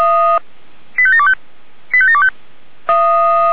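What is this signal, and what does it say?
Five-tone selective call (ZVEI), heard as quick runs of five tones, tones of about 70 ms. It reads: 85725 85725
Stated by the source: ZVEI five-tone selective call received on an SDR.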